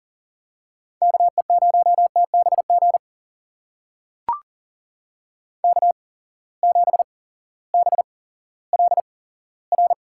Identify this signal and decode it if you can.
Morse code, heard as a keyed tone, about 39 wpm, spaced as for 18 wpm, 700 Hz. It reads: KE0TBG K7BLR
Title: USA Call Signs - 8x Character Spacing - 40wpm